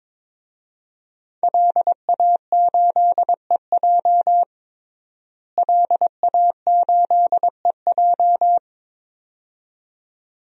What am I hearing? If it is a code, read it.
Morse code, heard as a keyed tone, about 22 wpm, 700 Hz. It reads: LA8EJ LA8EJ
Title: DXCC-100 Call Signs - 5 Characters - Repeated 2x - 22wpm